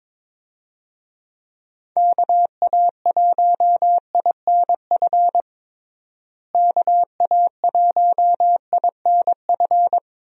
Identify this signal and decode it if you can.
Morse code, heard as a keyed tone, about 22 wpm, 700 Hz. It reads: KA1INF KA1INF
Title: USA Call Signs - 6 Characters - Repeated 3x - 22wpm